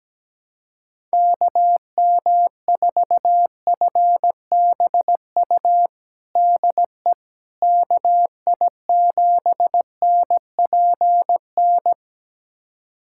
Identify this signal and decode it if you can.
Morse code, heard as a keyed tone, about 17 wpm, 700 Hz. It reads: KM4FBU DE KI7NPN
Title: USA Callsign DE Callsign 17wpm